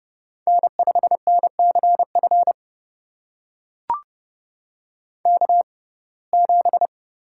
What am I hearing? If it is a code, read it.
Morse code, heard as a keyed tone, about 31 wpm, spaced as for 22 wpm, 700 Hz. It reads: N5NCF K7
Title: USA Call Signs - 5 Characters - 6x Character Spacing - 30wpm